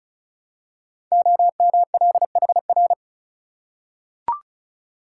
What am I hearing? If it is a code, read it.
Morse code, heard as a keyed tone, about 36 wpm, 700 Hz. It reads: OMLHR
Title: Code Groups - 5 Letters - 35wpm